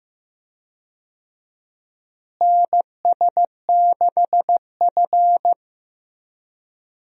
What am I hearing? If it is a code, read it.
Morse code, heard as a keyed tone, about 15 wpm, 700 Hz. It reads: NS6F
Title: USA Call Signs - 4 Characters - Repeated 3x - 15wpm